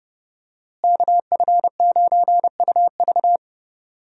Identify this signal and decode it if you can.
Morse code, heard as a keyed tone, about 30 wpm, 700 Hz. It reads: KF9UV
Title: USA Call Signs - 5 Characters - Repeated 3x - 30wpm